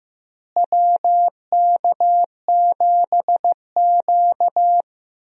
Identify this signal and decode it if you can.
Morse code, heard as a keyed tone, about 15 wpm, 700 Hz. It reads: WK7Q
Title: USA Call Signs - 4 Characters - Repeated 3x - 15wpm